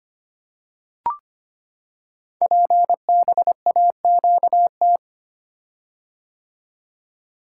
Morse code, heard as a keyed tone, about 25 wpm, 700 Hz. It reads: PBAQT
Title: Code Groups - 5 Letters - 25wpm